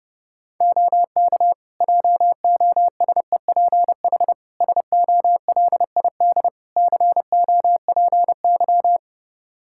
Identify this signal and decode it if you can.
Morse code, heard as a keyed tone, about 32 wpm, 700 Hz. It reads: OK JOSEPH SOLID COPY
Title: Practice QSO - Exchange 3 - More Info - 30wpm